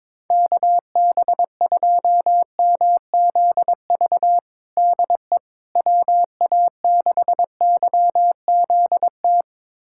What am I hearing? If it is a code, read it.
Morse code, heard as a keyed tone, about 22 wpm, 700 Hz. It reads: KB2MZV DE WA6YZT
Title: USA Callsign DE Callsign - 6 Characters - 22wpm